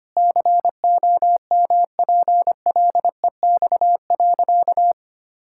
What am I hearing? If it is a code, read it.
Morse code, heard as a keyed tone, about 25 wpm, 700 Hz. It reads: COMPLEX.
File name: practice / 25wpm / Sentences with 4-Word Phrases - 25wpm